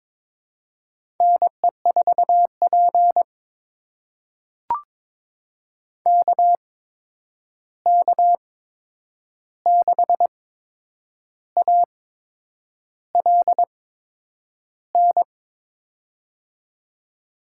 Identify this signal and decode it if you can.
Morse code, heard as a keyed tone, about 21 wpm, spaced as for 8 wpm, 700 Hz. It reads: NE4P KK6ALN
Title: USA Call Signs - 8x Character Spacing - 22wpm